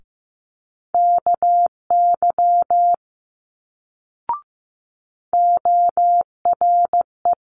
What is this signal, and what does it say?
Morse code, heard as a keyed tone, about 15 wpm, 700 Hz. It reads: KY ORE
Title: Multi-letter Phonemes Quick Fire 15wpm